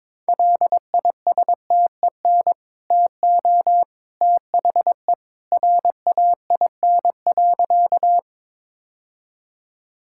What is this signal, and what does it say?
Morse code, heard as a keyed tone, about 22 wpm, 700 Hz. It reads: LISTEN TO THE RAIN.
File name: practice / 22wpm / Easy Sentences with Hints - 22wpm